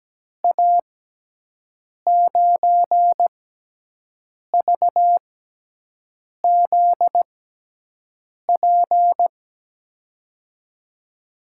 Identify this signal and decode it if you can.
Morse code, heard as a keyed tone, about 17 wpm, spaced as for 6 wpm, 700 Hz. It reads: A9VZP
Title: USA Call Signs - 6 Characters - 6x Character Spacing - 17wpm